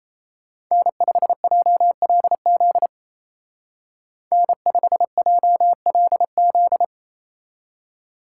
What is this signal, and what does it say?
Morse code, heard as a keyed tone, about 36 wpm, 700 Hz. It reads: N5JLZ N5JLZ